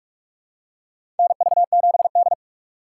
Morse code, heard as a keyed tone, about 42 wpm, 700 Hz. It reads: NV7D
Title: USA Call Signs - 3x Character Spacing - 45wpm